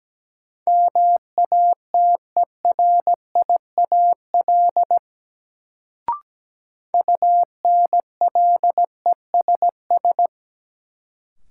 Morse code, heard as a keyed tone, about 17 wpm, 700 Hz. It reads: MATERIAL UNLESS
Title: Sentences from Top 1000 Words - Review Part 5 - New Word Drill - 17wpm